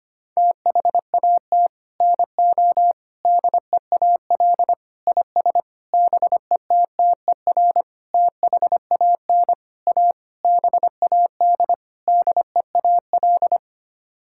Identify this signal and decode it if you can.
Morse code, heard as a keyed tone, about 25 wpm, 700 Hz. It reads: THAT NO DEAL IS BETTER THAN A BAD DEAL
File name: Sets of 9 Words - 25wpm